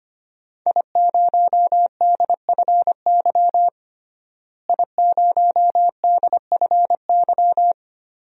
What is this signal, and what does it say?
Morse code, heard as a keyed tone, about 25 wpm, 700 Hz. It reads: I0DFY I0DFY
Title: DXCC-100 Call Signs - 5 Characters - Repeated 3x - 25wpm